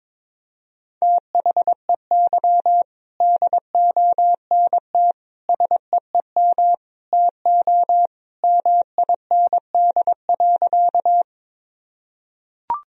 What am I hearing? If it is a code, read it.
Morse code, heard as a keyed tone, about 22 wpm, 700 Hz. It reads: THEY DONT SEEM TO MIND.
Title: Sentences from Top 1000 Words - Review Part 1 - 22wpm